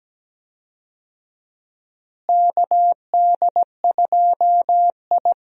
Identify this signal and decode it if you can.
Morse code, heard as a keyed tone, about 17 wpm, 700 Hz. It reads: KD2I
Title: USA Call Signs - 4 Characters - Repeated 2x - 17wpm